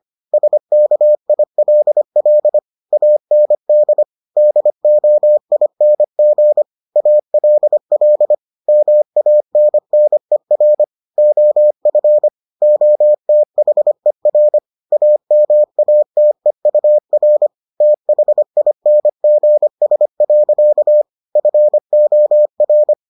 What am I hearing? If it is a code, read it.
Morse code, heard as a keyed tone, about 25 wpm, 600 Hz. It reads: SKILL AND DOING ALL MANNER OF OTHER AMATEUR THINGS. FOR